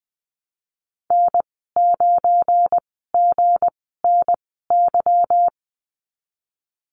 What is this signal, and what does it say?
Morse code, heard as a keyed tone, about 20 wpm, spaced as for 15 wpm, 700 Hz. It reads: N9GNY
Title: 193 Lesson 31g - Letter v - Call Signs 20wpm